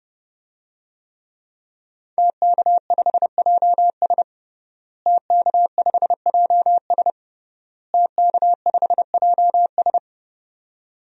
Morse code, heard as a keyed tone, about 32 wpm, 700 Hz. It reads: TK5JS TK5JS TK5JS
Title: DXCC-100 Call Signs - 5 Characters - Repeated 3x - 30wpm